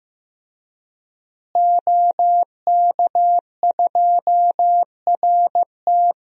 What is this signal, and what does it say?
Morse code, heard as a keyed tone, about 15 wpm, 700 Hz. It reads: OK2RT
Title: DXCC-100 Call Signs - 5 Characters - Repeated 3x - 15wpm